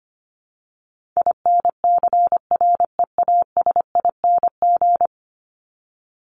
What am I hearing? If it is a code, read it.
Morse code, heard as a keyed tone, about 25 wpm, 700 Hz. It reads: INCREASING
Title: ING Suffix - Top 100 Words with Base Words 25wpm